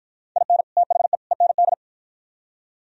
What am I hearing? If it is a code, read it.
Morse code, heard as a keyed tone, about 52 wpm, 700 Hz. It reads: IN THE END